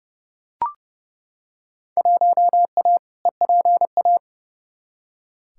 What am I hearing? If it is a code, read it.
Morse code, heard as a keyed tone, about 32 wpm, 700 Hz. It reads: 1A EPA
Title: Field Day Exchanges - 30wpm